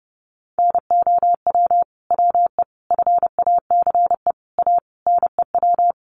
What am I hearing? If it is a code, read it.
Morse code, heard as a keyed tone, about 28 wpm, 700 Hz. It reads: NOW WE FACE A NEW
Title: Sentences from Top 1200 Words 30wpm